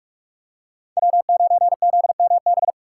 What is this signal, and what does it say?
Morse code, heard as a keyed tone, about 46 wpm, 700 Hz. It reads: W9ZMB
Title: USA Call Signs - 5 Characters - 3x Character Spacing - 45wpm